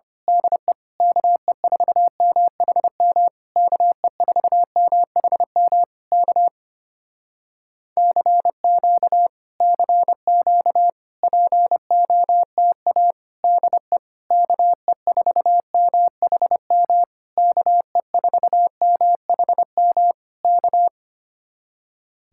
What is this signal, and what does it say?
Morse code, heard as a keyed tone, about 26 wpm, 700 Hz. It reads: DE KE4MHM KE4MHM K CQ CQ POTA DE KE4MHM KE4MHM K
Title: Speed-Racing Common QSO Phrases - 25wpm